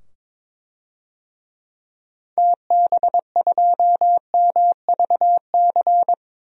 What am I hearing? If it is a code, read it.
Morse code, heard as a keyed tone, about 22 wpm, 700 Hz. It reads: TB2MVC
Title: DXCC-100 Call Signs - 6 Characters - 22wpm